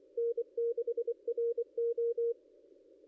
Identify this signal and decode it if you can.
Morse code, heard as a keyed tone, about 24 wpm, 470 Hz. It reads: N6RO
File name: contests / ss-cw-k5tr-op-n5rz -0800